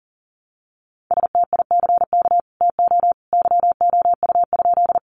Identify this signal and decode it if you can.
Morse code, heard as a keyed tone, about 40 wpm, 700 Hz. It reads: STICK TO YOU?
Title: Sentences from Top 1200 Words 40wpm